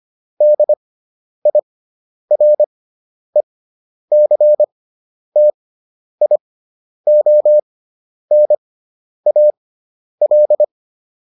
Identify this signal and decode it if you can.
Morse code, heard as a keyed tone, about 25 wpm, spaced as for 10 wpm, 600 Hz. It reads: DIRECTIONAL